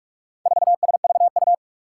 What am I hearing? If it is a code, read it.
Morse code, heard as a keyed tone, about 44 wpm, 700 Hz. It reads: 4SVU
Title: Four Challenging Letter-Number Combinations 45wpm